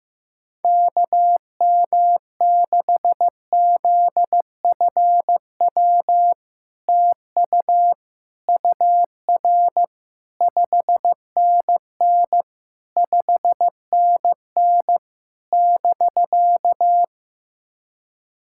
Morse code, heard as a keyed tone, about 15 wpm, 700 Hz. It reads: KM6ZFW TU UR 5NN 5NN <BK>
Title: SOTA QSO Practice 15wpm